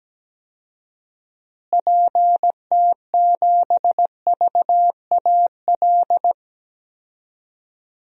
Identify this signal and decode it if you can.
Morse code, heard as a keyed tone, about 17 wpm, 700 Hz. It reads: PT7VAL
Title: DXCC-100 Call Signs - 6 Characters - Repeated 3x - 17wpm